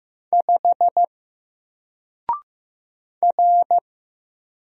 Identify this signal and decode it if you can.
Morse code, heard as a keyed tone, about 15 wpm, 700 Hz. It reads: 5 R
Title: Single Letter-Number 15wpm